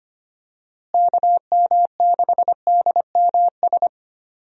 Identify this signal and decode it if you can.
Morse code, heard as a keyed tone, about 25 wpm, 700 Hz. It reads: KM6DMS
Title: USA Call Signs - 6 Characters - Repeated 3x - 25wpm